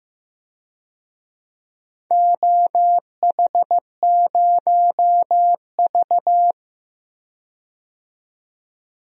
Morse code, heard as a keyed tone, about 15 wpm, 700 Hz. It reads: OH0V
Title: DXCC-100 Call Signs - 4 Characters - 15wpm